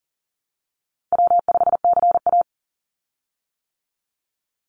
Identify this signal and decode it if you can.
Morse code, heard as a keyed tone, about 40 wpm, 700 Hz. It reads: W5CA
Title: USA Call Signs 40wpm